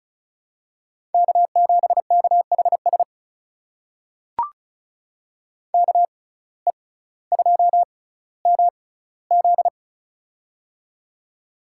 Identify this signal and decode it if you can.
Morse code, heard as a keyed tone, about 34 wpm, spaced as for 18 wpm, 700 Hz. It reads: K7KHS KE2MZ
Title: USA Call Signs - 5 Characters - 6x Character Spacing - 35wpm